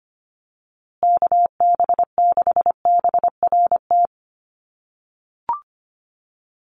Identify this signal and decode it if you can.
Morse code, heard as a keyed tone, about 25 wpm, 700 Hz. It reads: KB6BRT
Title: USA Call Signs Repeated 3x 25wpm